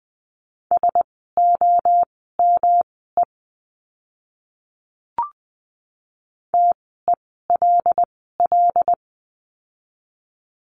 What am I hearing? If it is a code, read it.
Morse code, heard as a keyed tone, about 20 wpm, spaced as for 15 wpm, 700 Hz. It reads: SOME TELL
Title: 206 Lesson 33f - Letter j - Words 20wpm